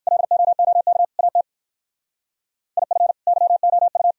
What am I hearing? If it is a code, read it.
Morse code, heard as a keyed tone, about 54 wpm, 700 Hz. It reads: LOOK AT IF YOU
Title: Speed-Racing - Sets of 4 Words 45wpm